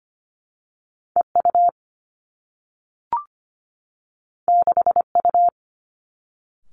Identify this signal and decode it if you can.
Morse code, heard as a keyed tone, about 25 wpm, 700 Hz. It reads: E U 6 U
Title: Two Challenging Letter-Number Combinations 25wpm